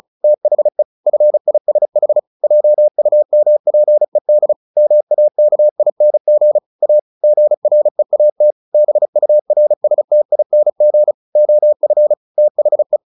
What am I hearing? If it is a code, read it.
Morse code, heard as a keyed tone, about 35 wpm, 600 Hz. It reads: THE FISH JUMPED MAKING A GREAT BURSTING OF THE
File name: the old man and the sea 2 35wpm